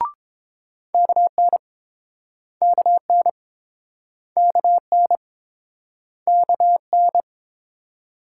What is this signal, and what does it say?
Morse code, heard as a keyed tone, about 27 wpm, 700 Hz. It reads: KN KN KN KN
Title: Speed-Racing Multi-letter Phonemes - 22wpm